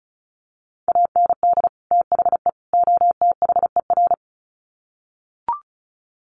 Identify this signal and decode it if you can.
Morse code, heard as a keyed tone, about 34 wpm, 700 Hz. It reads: AND THE OTHER
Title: Sets of 3 Words 35wpm